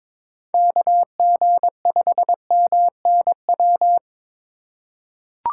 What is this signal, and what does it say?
Morse code, heard as a keyed tone, about 22 wpm, 700 Hz. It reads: KG5MNW
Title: USA Call Signs - 6 Characters - 2x Character Spacing - 22wpm